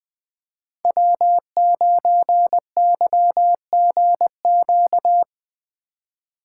Morse code, heard as a keyed tone, about 21 wpm, 700 Hz. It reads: W9YGQ
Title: Speed Racing - USA 1x3 Call Signs 20wpm